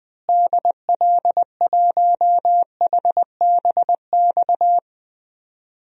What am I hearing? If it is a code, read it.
Morse code, heard as a keyed tone, about 20 wpm, 700 Hz. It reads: DL1HBX
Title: DXCC-100 Call Signs - 6 Characters - Repeated 2x - 20wpm